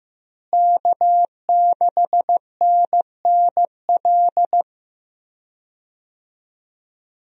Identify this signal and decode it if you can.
Morse code, heard as a keyed tone, about 15 wpm, 700 Hz. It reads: K6NNL
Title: USA Call Signs - 5 Characters - Repeated 3x - 15wpm